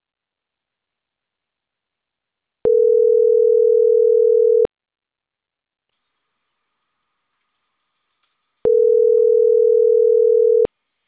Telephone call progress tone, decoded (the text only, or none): ringback tone